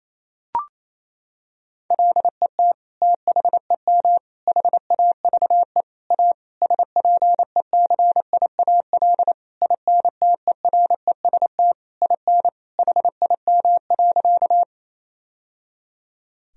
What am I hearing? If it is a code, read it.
Morse code, heard as a keyed tone, about 27 wpm, 700 Hz. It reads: LET THEM HAVE A SPECIAL INTEREST IN HIM.
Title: Sentences from Top 1000 Words - Review Part 3 - 28wpm